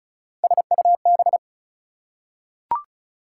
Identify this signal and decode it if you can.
Morse code, heard as a keyed tone, about 36 wpm, 700 Hz. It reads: SUB
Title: Prefixes and Suffixes 35wpm